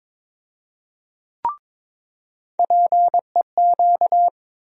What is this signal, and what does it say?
Morse code, heard as a keyed tone, about 22 wpm, 700 Hz. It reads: PEQ
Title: USA Call Sign Postfixes - 3 Letters 22wpm